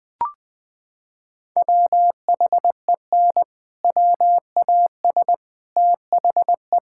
Morse code, heard as a keyed tone, about 20 wpm, 700 Hz. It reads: WHEN WAS THE
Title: Sets of 6 Words - 20wpm